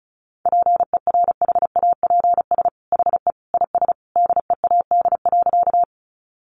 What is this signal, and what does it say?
Morse code, heard as a keyed tone, about 34 wpm, 700 Hz. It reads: PERHAPS HE IS DEAD.
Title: Sentences from Top 1500 Words 35wpm